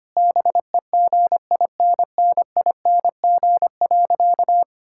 Morse code, heard as a keyed tone, about 25 wpm, 700 Hz. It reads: BEGINNING.
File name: practice / 25wpm / Sentences with 3-Word Phrases - 25wpm